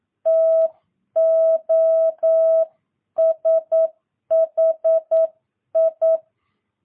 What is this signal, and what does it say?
Morse code, heard as a keyed tone, about 9 wpm, 650 Hz. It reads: TOSHI
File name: jh8cbh / 2022cw